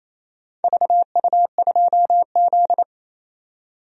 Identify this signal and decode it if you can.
Morse code, heard as a keyed tone, about 28 wpm, 700 Hz. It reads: VU2Z
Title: DXCC-100 Call Signs - 4 Characters - 28wpm